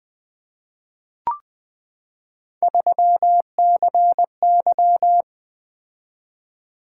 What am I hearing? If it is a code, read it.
Morse code, heard as a keyed tone, about 20 wpm, 700 Hz. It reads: 3CY
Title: Three Letter-Number Combinations - 20wpm